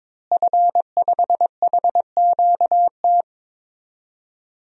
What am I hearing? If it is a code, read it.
Morse code, heard as a keyed tone, about 22 wpm, 700 Hz. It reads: F5HQT
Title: DXCC-100 Call Signs - 5 Characters - Repeated 2x - 22wpm